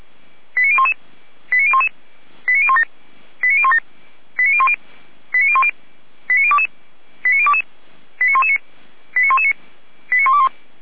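Five-tone selective call (ZVEI), heard as quick runs of five tones, tones of about 70 ms. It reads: 89011 89011 89017 89017 89010 89010 89022 89022 89119 89119 89121
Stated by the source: Received on SDR, ZVEI five-tone selective call 89060.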